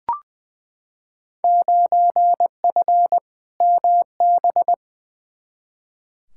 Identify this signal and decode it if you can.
Morse code, heard as a keyed tone, about 21 wpm, 700 Hz. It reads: 9F MB
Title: Field Day Exchanges Encore - 20wpm